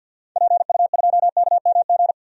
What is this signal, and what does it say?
Morse code, heard as a keyed tone, about 52 wpm, 700 Hz. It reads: PU2KMG
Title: DXCC-100 Call Signs - 6 Characters - 50wpm